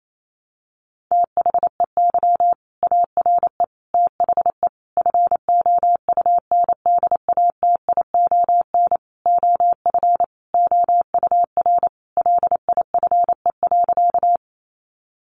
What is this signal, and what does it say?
Morse code, heard as a keyed tone, about 28 wpm, 700 Hz. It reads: THEY ARE THE FOUNDATION OF OUR LIFE.
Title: Sentences from Top 1200 Words 28wpm